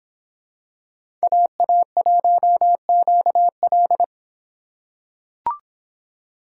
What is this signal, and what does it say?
Morse code, heard as a keyed tone, about 23 wpm, 700 Hz. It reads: AA1QL